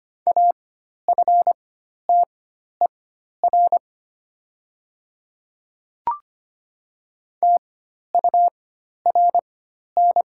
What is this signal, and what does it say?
Morse code, heard as a keyed tone, about 25 wpm, spaced as for 12 wpm, 700 Hz. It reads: AFTER TURN
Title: 4x Character Spacing - Top 300 Words - 25wpm